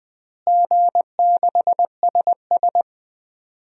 Morse code, heard as a keyed tone, about 20 wpm, 700 Hz. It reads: G6SS
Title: DXCC-100 Call Signs - 4 Characters - Repeated 3x - 20wpm